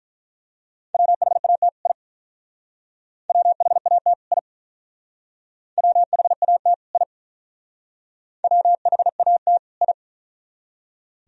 Morse code, heard as a keyed tone, about 44 wpm, 700 Hz. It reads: WHAT I WHAT I WHAT I WHAT I